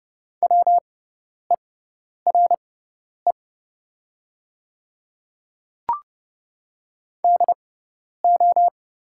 Morse code, heard as a keyed tone, about 30 wpm, spaced as for 10 wpm, 700 Hz. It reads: WERE DO
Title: 6x Character Spacing - Top 100 Words - 30wpm